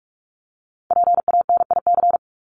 Morse code, heard as a keyed tone, about 44 wpm, 700 Hz. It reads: PANIC
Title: Top 1751-2000 Words 45wpm